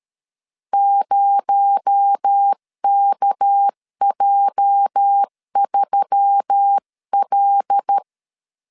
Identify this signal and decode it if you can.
Morse code, heard as a keyed tone, about 13 wpm, 800 Hz. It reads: 0KJ3L
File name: LEZIONI / 10 SIGNS / 10B INT 12wpm 2414